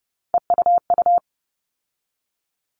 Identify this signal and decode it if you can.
Morse code, heard as a keyed tone, about 30 wpm, 700 Hz. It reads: EUU
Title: All Cut Numbers - 3-Digit 30wpm